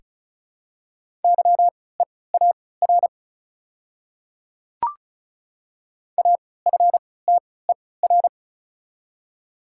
Morse code, heard as a keyed tone, about 36 wpm, spaced as for 20 wpm, 700 Hz. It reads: YEAR AFTER
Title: 3x Character Spacing - Top 200 Words - 35wpm